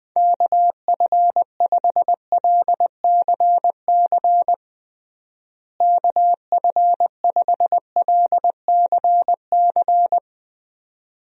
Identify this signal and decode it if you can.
Morse code, heard as a keyed tone, about 21 wpm, 700 Hz. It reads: KF5LCC KF5LCC
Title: USA Call Signs - 6 Characters - Repeated 2x - 20wpm